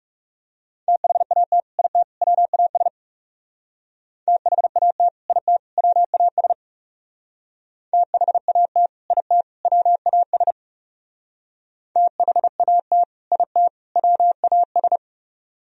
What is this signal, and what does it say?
Morse code, heard as a keyed tone, about 40 wpm, 700 Hz. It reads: THAT IT WAS THAT IT WAS THAT IT WAS THAT IT WAS